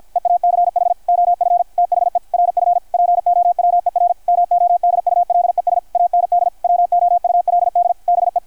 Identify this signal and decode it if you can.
Morse code, heard as a keyed tone, about 51 wpm, 660 Hz, with radio noise. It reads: EACH OF THE RF POWER MODULES AND WOULD BE
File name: archived / CRSnet